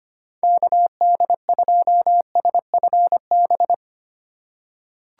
Morse code, heard as a keyed tone, about 25 wpm, 700 Hz. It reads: KD2SFB
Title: USA Call Signs - 3x Character Spacing - 25wpm